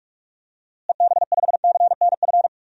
Morse code, heard as a keyed tone, about 47 wpm, 700 Hz. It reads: EB5CNF
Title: DXCC-100 Call Signs - 6 Characters - Repeated 2x - 45wpm